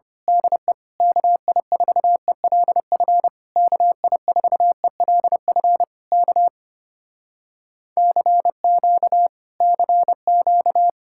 Morse code, heard as a keyed tone, about 26 wpm, 700 Hz. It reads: DE KI4ELF KI4ELF K CQ CQ
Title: Speed-Racing Common QSO Phrases - 25wpm